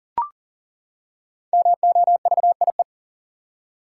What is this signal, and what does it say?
Morse code, heard as a keyed tone, about 40 wpm, 700 Hz. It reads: MOVIE
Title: Five Letter Words - 40wpm